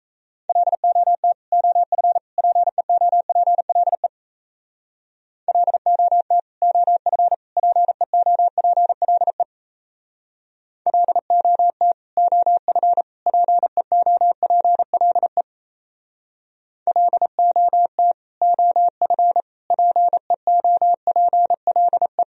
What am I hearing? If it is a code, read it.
Morse code, heard as a keyed tone, about 35 wpm, 700 Hz. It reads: LOT OF PEOPLE LOT OF PEOPLE LOT OF PEOPLE LOT OF PEOPLE
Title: Speed-Racing - Sets of 3 Words 28wpm